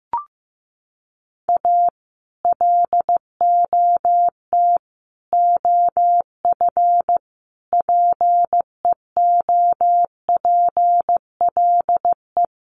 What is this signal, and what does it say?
Morse code, heard as a keyed tone, about 15 wpm, 700 Hz. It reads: A LOT OF PEOPLE
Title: Sets of 5 Words 15wpm